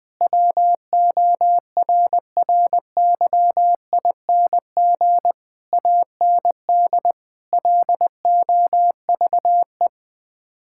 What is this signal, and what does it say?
Morse code, heard as a keyed tone, about 20 wpm, 700 Hz. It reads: WORRYING AND LOVE
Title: Sets of 8 Words - 20wpm